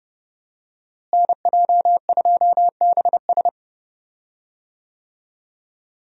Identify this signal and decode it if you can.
Morse code, heard as a keyed tone, about 30 wpm, 700 Hz. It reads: NJ2BS
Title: USA Call Signs - 5 Characters - Repeated 3x - 30wpm